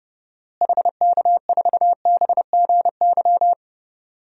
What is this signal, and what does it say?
Morse code, heard as a keyed tone, about 30 wpm, 700 Hz. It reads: HK4BGY
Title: DXCC-100 Call Signs - 6 Characters - Repeated 3x - 30wpm